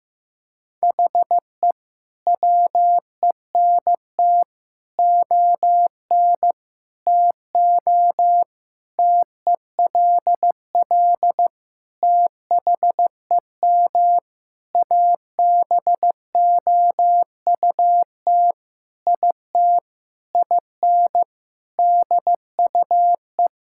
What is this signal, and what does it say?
Morse code, heard as a keyed tone, about 15 wpm, 700 Hz. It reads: HE WENT ON TO TELL THEM ABOUT IT IN DUE
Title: Sentences from Top 1000 Words - Review Part 6 - 15wpm